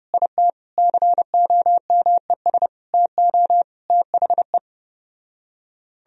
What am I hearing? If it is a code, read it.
Morse code, heard as a keyed tone, about 33 wpm, 700 Hz. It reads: IT COMES TO THE